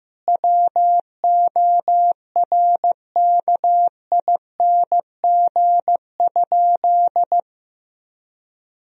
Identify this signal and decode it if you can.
Morse code, heard as a keyed tone, about 15 wpm, 700 Hz. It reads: WORKING?
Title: Sentences from Top 1000 Words - Review Part 4 - 15wpm